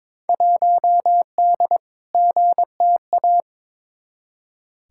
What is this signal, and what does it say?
Morse code, heard as a keyed tone, about 22 wpm, 700 Hz. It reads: 1D GTA
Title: Field Day Exchanges - 22wpm